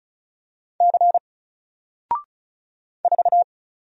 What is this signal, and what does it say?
Morse code, heard as a keyed tone, about 34 wpm, 700 Hz. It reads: C 4
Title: Single Character (Letter:Number:Punctuation:Prosign) Alt 35wpm